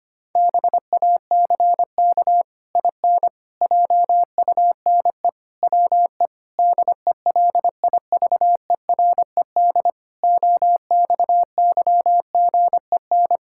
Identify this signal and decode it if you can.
Morse code, heard as a keyed tone, about 25 wpm, 700 Hz. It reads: BACK IN JUNE WE DELIVERED OXYGEN
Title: Pangram Challenge v2 - 25wpm